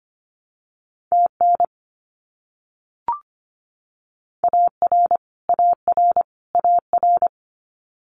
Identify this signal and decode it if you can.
Morse code, heard as a keyed tone, about 25 wpm, 700 Hz. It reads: TN AR AR AR
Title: US States Repeated 3x 25wpm